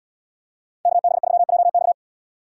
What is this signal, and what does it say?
Morse code, heard as a keyed tone, about 75 wpm, 700 Hz. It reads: KB3JX